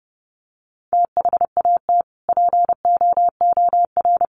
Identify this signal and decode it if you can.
Morse code, heard as a keyed tone, about 30 wpm, 700 Hz. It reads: THAT POOR
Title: Sentences from Top 1100 Words 30wpm